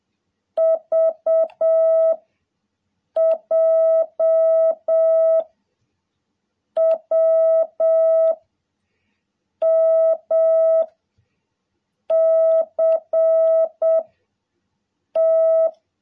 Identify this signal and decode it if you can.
Morse code, heard as a keyed tone, about 7 wpm, 650 Hz. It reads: V J W M C T
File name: jh8cbh / 2021cw